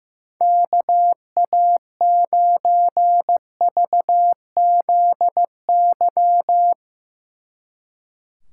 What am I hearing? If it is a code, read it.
Morse code, heard as a keyed tone, about 16 wpm, 700 Hz. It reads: KA9VZY
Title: Speed Racing - USA 2x3 Call Signs 15wpm